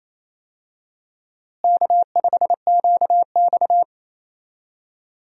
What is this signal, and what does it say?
Morse code, heard as a keyed tone, about 28 wpm, 700 Hz. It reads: K5QX
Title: USA Call Signs - 4 Characters - Repeated 3x - 28wpm